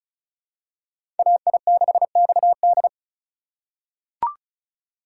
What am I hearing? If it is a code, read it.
Morse code, heard as a keyed tone, about 34 wpm, 700 Hz. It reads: AI6XD
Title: USA Call Signs - 5 Characters - 2x Character Spacing - 35wpm